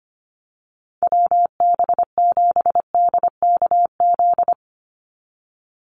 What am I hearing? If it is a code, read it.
Morse code, heard as a keyed tone, about 25 wpm, 700 Hz. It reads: WB7DKZ
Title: Call Signs - 25wpm